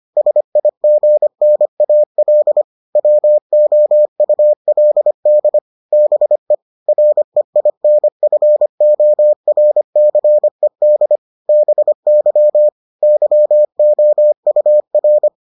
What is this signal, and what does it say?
Morse code, heard as a keyed tone, about 25 wpm, 600 Hz. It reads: SIGNAL WOULD BE REINFORCED BY YOUR